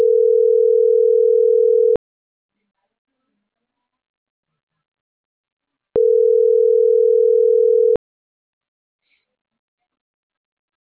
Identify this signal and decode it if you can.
Ringback tone (North America), 440 + 480 Hz.